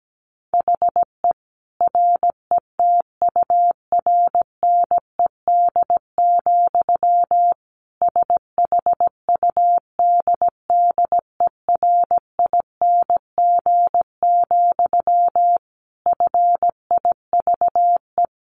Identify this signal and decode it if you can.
Morse code, heard as a keyed tone, about 17 wpm, 700 Hz. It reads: HE RETURNED, SHUDDERING, FIVE
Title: A Sound Of Thunder 17wpm